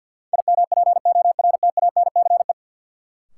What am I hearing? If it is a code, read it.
Morse code, heard as a keyed tone, about 53 wpm, 700 Hz. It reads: IMPORTANCE
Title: Sentences from Top 1200 Words - New Word Drill Alt - 50wpm